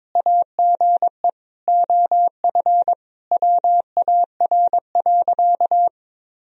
Morse code, heard as a keyed tone, about 21 wpm, 700 Hz. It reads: AGE OF WAR.
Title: Sentences from Top 1000 Words - Review Part 3 - 22wpm